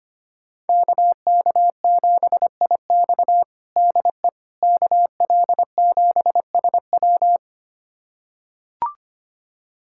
Morse code, heard as a keyed tone, about 25 wpm, 700 Hz. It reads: KK7IX DE KL7SW
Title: USA Callsign DE Callsign - 5 Characters - 25wpm